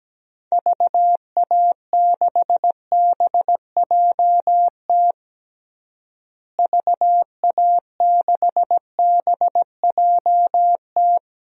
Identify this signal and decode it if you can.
Morse code, heard as a keyed tone, about 17 wpm, 700 Hz. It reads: VA6BJT VA6BJT
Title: DXCC-100 Call Signs - 6 Characters - Repeated 2x - 17wpm